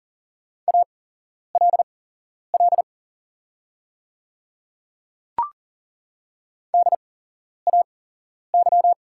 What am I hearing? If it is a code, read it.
Morse code, heard as a keyed tone, about 40 wpm, spaced as for 11 wpm, 690 Hz. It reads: ALL DAY
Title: 8x Character Spacing - Top 100 Words - 40wpm